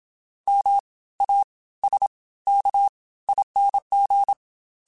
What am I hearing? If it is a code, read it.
Morse code, heard as a keyed tone, about 27 wpm, spaced as for 18 wpm, 800 Hz. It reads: MASKING